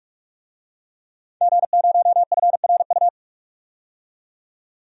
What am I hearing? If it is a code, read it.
Morse code, heard as a keyed tone, about 46 wpm, 700 Hz. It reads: G0FRU
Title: DXCC-100 Call Signs - 5 Characters - 45wpm